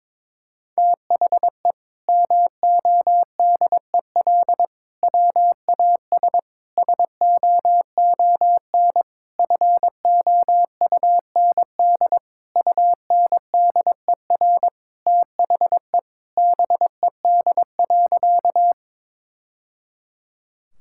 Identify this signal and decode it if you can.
Morse code, heard as a keyed tone, about 22 wpm, 700 Hz. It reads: THE MODEL WAS SOON FOUND UNDER THE BED.
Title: Sentences from Top 1000 Words - Review Part 5 - 22wpm